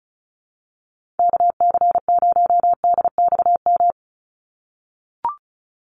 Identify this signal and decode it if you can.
Morse code, heard as a keyed tone, about 34 wpm, 700 Hz. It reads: KC0DXM
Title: USA Call Signs 35wpm